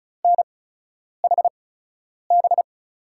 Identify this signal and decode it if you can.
Morse code, heard as a keyed tone, about 36 wpm, spaced as for 10 wpm, 700 Hz. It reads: NHB
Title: USA Call Signs - 8x Character Spacing - 35wpm